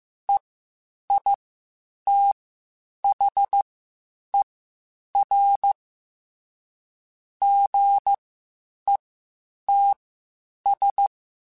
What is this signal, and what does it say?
Morse code, heard as a keyed tone, about 15 wpm, spaced as for 8 wpm, 800 Hz. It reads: EITHER GETS
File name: Spaced / Chuckles Spaced-mp3s / Squeaky sp @13